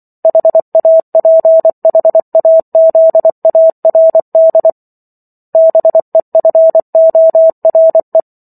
Morse code, heard as a keyed tone, about 24 wpm, 650 Hz. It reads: HAPHAZARD BEFORE